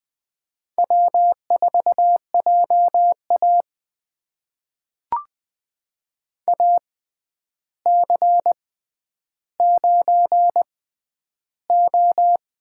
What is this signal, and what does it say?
Morse code, heard as a keyed tone, about 21 wpm, spaced as for 10 wpm, 700 Hz. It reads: W4JA AC9O
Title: USA Call Signs - 4 Characters - 6x Character Spacing - 20wpm